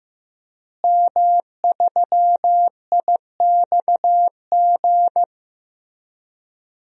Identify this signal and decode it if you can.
Morse code, heard as a keyed tone, about 15 wpm, 700 Hz. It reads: M3IXG
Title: DXCC-100 Call Signs - 5 Characters - Repeated 3x - 15wpm